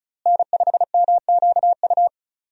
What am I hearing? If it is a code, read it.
Morse code, heard as a keyed tone, about 34 wpm, 700 Hz. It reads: N5MQU
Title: USA Call Signs - 5 Characters - Repeated 2x - 35wpm